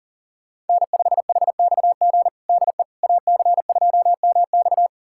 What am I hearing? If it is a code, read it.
Morse code, heard as a keyed tone, about 40 wpm, 690 Hz. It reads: N5HXG DE AC2MX
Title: USA Callsign DE Callsign - 5 Characters - 40wpm